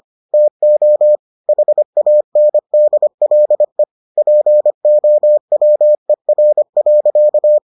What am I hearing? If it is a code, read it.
Morse code, heard as a keyed tone, about 25 wpm, 600 Hz. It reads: TO HANDLE POWER.